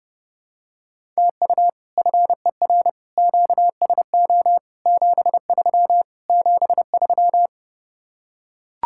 Morse code, heard as a keyed tone, about 30 wpm, 700 Hz. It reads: TU FER QSO 73 73